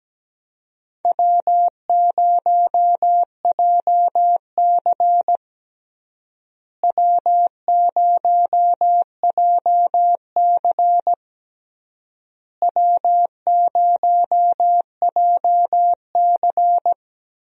Morse code, heard as a keyed tone, about 17 wpm, 700 Hz. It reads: W0JC W0JC W0JC
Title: USA Call Signs - 4 Characters - Repeated 3x - 17wpm